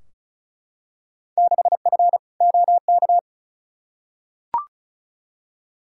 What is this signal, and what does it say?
Morse code, heard as a keyed tone, about 34 wpm, 700 Hz. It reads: 6F OK
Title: Field Day Exchanges Encore - 35wpm